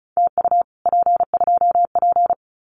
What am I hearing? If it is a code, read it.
Morse code, heard as a keyed tone, about 35 wpm, 700 Hz. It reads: TU P2P
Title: Common QSO Phrases Encore 35wpm